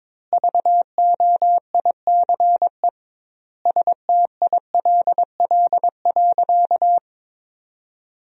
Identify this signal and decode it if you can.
Morse code, heard as a keyed tone, about 22 wpm, 700 Hz. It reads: VOICE STILL.
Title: 2x Word-Spacing - Sentences from Top 500 Words - 22wpm